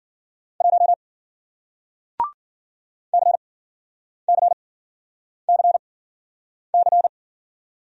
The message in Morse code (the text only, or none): 1 CCCC